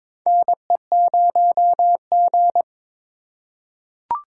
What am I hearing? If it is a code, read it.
Morse code, heard as a keyed tone, about 22 wpm, 700 Hz. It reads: NE0G
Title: USA Call Signs - 4 Characters - 4x Character Spacing - 22wpm